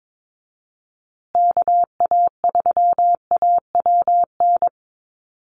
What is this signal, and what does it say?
Morse code, heard as a keyed tone, about 22 wpm, 700 Hz. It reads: KA3AWN
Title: USA Call Signs Repeated 2x 22wpm